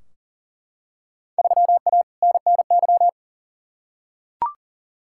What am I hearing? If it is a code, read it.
Morse code, heard as a keyed tone, about 40 wpm, 700 Hz. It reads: 3A NNY
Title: Field Day Exchanges - 40wpm